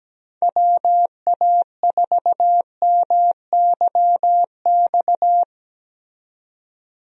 Morse code, heard as a keyed tone, about 17 wpm, 700 Hz. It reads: WA4MYX
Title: USA Call Signs - 6 Characters - 17wpm